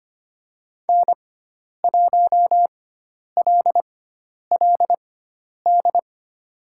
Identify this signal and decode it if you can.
Morse code, heard as a keyed tone, about 25 wpm, spaced as for 10 wpm, 700 Hz. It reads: N1LLD
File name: Lesson 16g - Letter c - Call Signs 25wpm